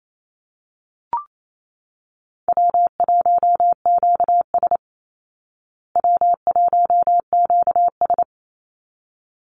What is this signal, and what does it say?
Morse code, heard as a keyed tone, about 28 wpm, 700 Hz. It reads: W1QS W1QS
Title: Call Signs Repeated 3x - 28wpm